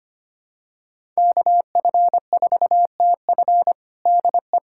Morse code, heard as a keyed tone, about 25 wpm, 700 Hz. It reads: KF4TF DE
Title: USA Callsign DE Callsign - 5 Characters - 25wpm